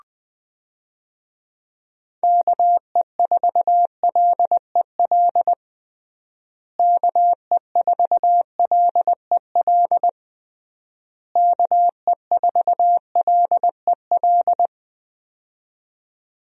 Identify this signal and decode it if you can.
Morse code, heard as a keyed tone, about 20 wpm, 700 Hz. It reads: KE4LEL KE4LEL KE4LEL
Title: USA Call Signs - 6 Characters - Repeated 3x - 20wpm